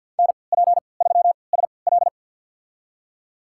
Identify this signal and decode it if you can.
Morse code, heard as a keyed tone, about 52 wpm, spaced as for 29 wpm, 700 Hz. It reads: NP3SL
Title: USA Call Signs - 5 Characters - 3x Character Spacing - 50wpm